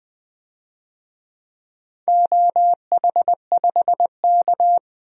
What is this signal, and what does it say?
Morse code, heard as a keyed tone, about 20 wpm, 700 Hz. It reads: OH5K
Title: DXCC-100 Call Signs - 4 Characters - 20wpm